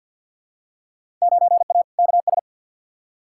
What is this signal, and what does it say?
Morse code, heard as a keyed tone, about 51 wpm, 700 Hz. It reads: 9A KS
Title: Field Day Exchanges Encore - 50wpm